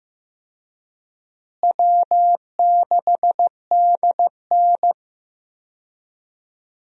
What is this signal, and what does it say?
Morse code, heard as a keyed tone, about 15 wpm, 700 Hz. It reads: W6DN
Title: USA Call Signs - 4 Characters - Repeated 2x - 15wpm